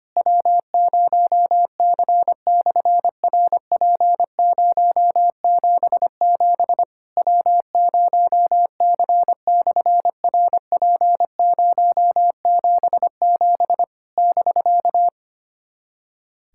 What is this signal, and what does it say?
Morse code, heard as a keyed tone, about 25 wpm, 700 Hz. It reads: W0C/RP077 W0C/RP077 <BK>
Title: SOTA QSO Practice 25wpm